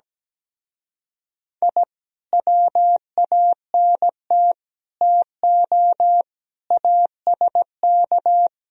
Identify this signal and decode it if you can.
Morse code, heard as a keyed tone, about 17 wpm, 700 Hz. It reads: I WANT TO ASK
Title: Speed-Racing - Sets of 5 Words 17wpm